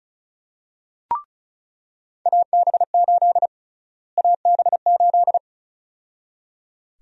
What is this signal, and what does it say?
Morse code, heard as a keyed tone, about 36 wpm, 700 Hz. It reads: AB8 AB8
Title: USA Call Sign Prefixes - Repeated 2x - 35wpm